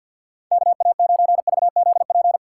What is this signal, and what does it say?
Morse code, heard as a keyed tone, about 50 wpm, 700 Hz. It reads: KA9VZP
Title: USA Call Signs - 6 Characters - 3x Character Spacing - 50wpm